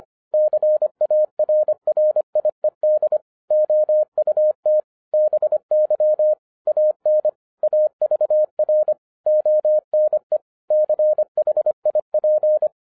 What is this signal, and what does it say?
Morse code, heard as a keyed tone, about 25 wpm, 610 Hz. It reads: CARRIED OUT BY AN AVR ONE CHIP
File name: 2011-10-mp3 / n09-CW TO PS2 CONVERTER 25wpm